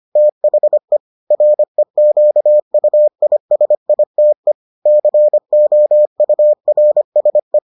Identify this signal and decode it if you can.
Morse code, heard as a keyed tone, about 25 wpm, 600 Hz. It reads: THE REQUISITE COURSE